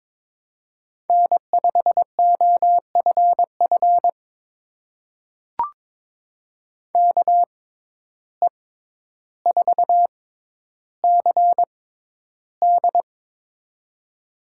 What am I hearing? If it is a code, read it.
Morse code, heard as a keyed tone, about 22 wpm, spaced as for 11 wpm, 700 Hz. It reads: N5OFF KE4CD
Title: USA Call Signs - 5 Characters - 6x Character Spacing - 22wpm